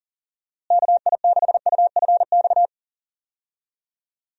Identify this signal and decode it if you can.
Morse code, heard as a keyed tone, about 40 wpm, 700 Hz. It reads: KI6UFX
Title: USA Call Signs - 6 Characters - Repeated 2x - 40wpm